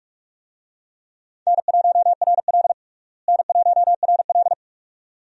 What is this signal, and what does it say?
Morse code, heard as a keyed tone, about 45 wpm, 700 Hz. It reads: N1RL N1RL
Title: USA Call Signs - 4 Characters - Repeated 2x - 45wpm